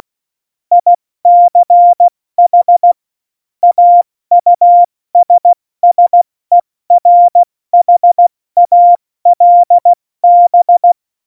Morse code, heard as a keyed tone, about 16 wpm, 700 Hz. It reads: ICH AUSSERHALB
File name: Morsekurs / Witze / Witz 29. September 2020, 70 F80